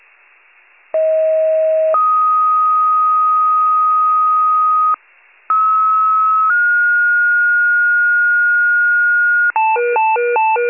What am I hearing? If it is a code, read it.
Two-tone page, heard as a one-second tone, then a three-second tone, about 640 then 1250 Hz, about 1290 then 1430 Hz. It is heard from 0.9 to 9.5 s.
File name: wcalert